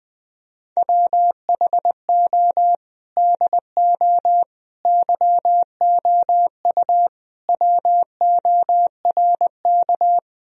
Morse code, heard as a keyed tone, about 20 wpm, 700 Hz. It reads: WHO DO YOU WORK
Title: Sentences from Top 1000 Words - Review Part 4 - 20wpm